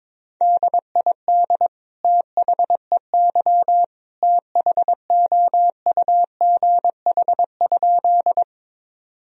Morse code, heard as a keyed tone, about 22 wpm, 700 Hz. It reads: DID THEY THOUGH?
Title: Sentences from Top 1000 Words - Review Part 2 - 22wpm